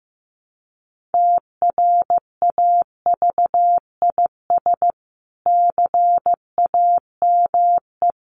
Morse code, heard as a keyed tone, about 15 wpm, 700 Hz. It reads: TRAVIS CAME
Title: A Sound Of Thunder 15wpm